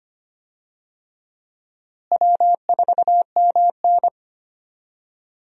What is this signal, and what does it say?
Morse code, heard as a keyed tone, about 25 wpm, 700 Hz. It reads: W4MN
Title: USA Call Signs - 4 Characters - Repeated 3x - 25wpm